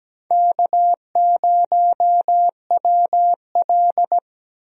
Morse code, heard as a keyed tone, about 17 wpm, 700 Hz. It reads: K0WL
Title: USA Call Signs - 4 Characters - Repeated 3x - 17wpm